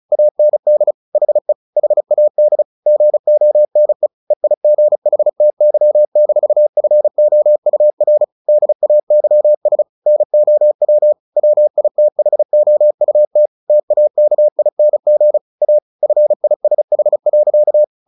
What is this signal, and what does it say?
Morse code, heard as a keyed tone, about 34 wpm, 600 Hz. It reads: AND HE HAD GONE EIGHTY-FOUR DAYS NOW WITHOUT TAKING A FISH.
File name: the old man and the sea / the old man and the sea 1 35wpm